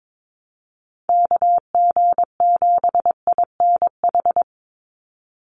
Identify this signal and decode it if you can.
Morse code, heard as a keyed tone, about 22 wpm, 700 Hz. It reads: KG7INH
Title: USA Call Signs 22wpm